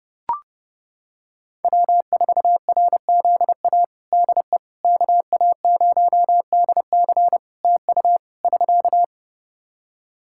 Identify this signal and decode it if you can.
Morse code, heard as a keyed tone, about 30 wpm, 700 Hz. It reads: W4RZA DE KA0DC TU <SK>
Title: Practice QSO - Exchange 4 - Ending - 30wpm